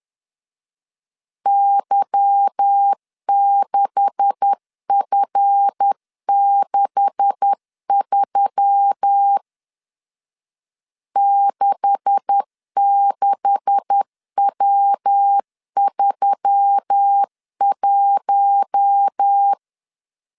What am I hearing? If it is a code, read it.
Morse code, heard as a keyed tone, about 10 wpm, 800 Hz. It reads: Y6F63 66W31
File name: LEZIONI / 9 286 10wpm 2735